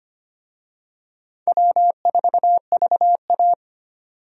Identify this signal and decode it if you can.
Morse code, heard as a keyed tone, about 25 wpm, 700 Hz. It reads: W4VA
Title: USA Call Signs - 4 Characters - 25wpm